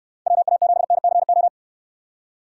Morse code, heard as a keyed tone, about 68 wpm, 700 Hz. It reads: WA7NCO